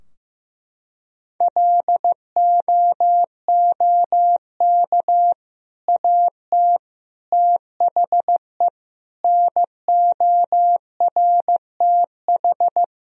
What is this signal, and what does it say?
Morse code, heard as a keyed tone, about 15 wpm, 700 Hz. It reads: LOOK AT THE NORTH
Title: Sentences from Top 1000 Words - Review Part 2 - 15wpm